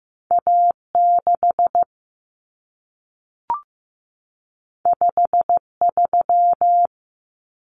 Morse code, heard as a keyed tone, about 15 wpm, 700 Hz. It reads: A6 53
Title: Common Cut Numbers - 2-Digit 15wpm